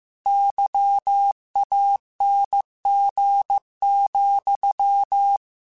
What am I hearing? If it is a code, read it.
Morse code, heard as a keyed tone, about 15 wpm, 800 Hz. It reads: YANG,
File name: Misc / Misc-mp3s / Word Pairs @13